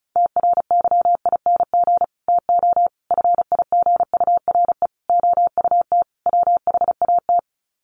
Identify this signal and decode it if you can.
Morse code, heard as a keyed tone, about 34 wpm, 700 Hz. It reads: TRYING TO FIGURE OUT WHAT
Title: Sets of 5 Words 35wpm